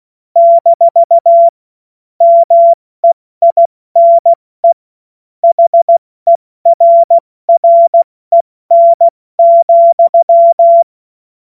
Morse code, heard as a keyed tone, about 16 wpm, 680 Hz. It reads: - MEINE HERREN,
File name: Morsekurs / Witze / Witz 06. September 2022 70 F80